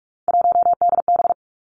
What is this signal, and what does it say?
Morse code, heard as a keyed tone, about 46 wpm, 700 Hz. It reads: 1DB
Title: Three Challenging Letter-Number Combinations 45wpm